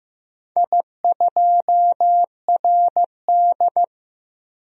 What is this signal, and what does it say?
Morse code, heard as a keyed tone, about 15 wpm, 700 Hz. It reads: I2RD
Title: DXCC-100 Call Signs - 4 Characters - Repeated 3x - 15wpm